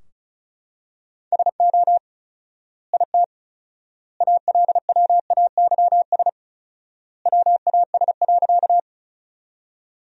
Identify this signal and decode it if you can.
Morse code, heard as a keyed tone, about 36 wpm, 700 Hz. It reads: SO IT ALWAYS WAS.
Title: 4x Word-Spacing - Sentences from Top 300 Words - 35wpm